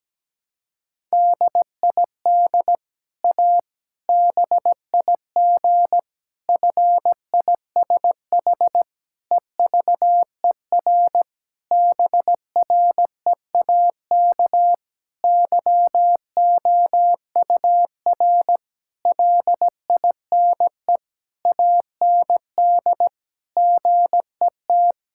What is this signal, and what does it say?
Morse code, heard as a keyed tone, about 17 wpm, 700 Hz. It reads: DID A BIG FISH EVER BREAK YOUR LINE AND GET
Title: Sentences from Top 1000 Words - Review Part 6 - 17wpm